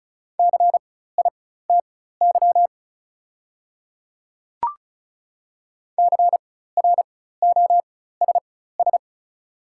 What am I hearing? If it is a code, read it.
Morse code, heard as a keyed tone, about 36 wpm, spaced as for 16 wpm, 700 Hz. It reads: CITY CROSS
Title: 4x Character Spacing - Top 300 Words - 35wpm